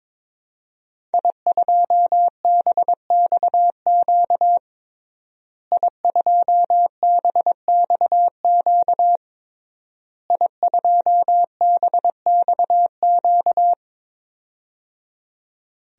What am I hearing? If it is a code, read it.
Morse code, heard as a keyed tone, about 22 wpm, 700 Hz. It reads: I2BXQ I2BXQ I2BXQ
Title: DXCC-100 Call Signs - 5 Characters - Repeated 3x - 22wpm